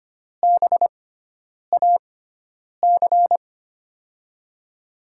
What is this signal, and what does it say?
Morse code, heard as a keyed tone, about 25 wpm, spaced as for 9 wpm, 700 Hz. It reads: BAC
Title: USA Call Signs - 6 Characters - 6x Character Spacing - 25wpm